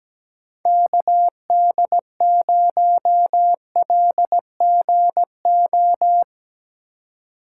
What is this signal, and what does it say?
Morse code, heard as a keyed tone, about 17 wpm, 700 Hz. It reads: KD0LGO
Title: USA Call Signs - 6 Characters - 3x Character Spacing - 17wpm